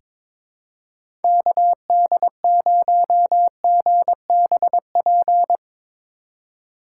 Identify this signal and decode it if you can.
Morse code, heard as a keyed tone, about 21 wpm, 700 Hz. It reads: KD0GBP
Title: USA Call Signs - 6 Characters - 2x Character Spacing - 22wpm